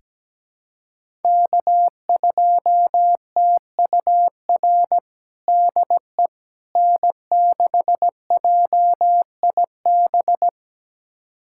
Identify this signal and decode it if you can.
Morse code, heard as a keyed tone, about 17 wpm, 700 Hz. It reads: K2TUR DE N6JIB
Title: USA Callsign DE Callsign - 5 Characters - 17wpm